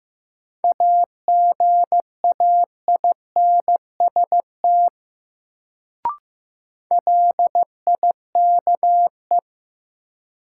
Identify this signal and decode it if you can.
Morse code, heard as a keyed tone, about 15 wpm, 700 Hz. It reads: AGAINST LIKE
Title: Sets of 2 Words - New Word Drill - 15wpm